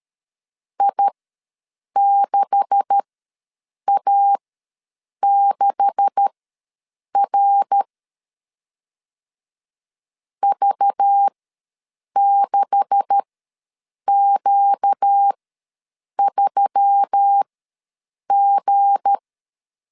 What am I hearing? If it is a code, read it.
Morse code, heard as a keyed tone, about 13 wpm, spaced as for 7 wpm, 800 Hz. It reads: I6A6R V6Q3G
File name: LEZIONI / 9 286 ws 12wpm 3351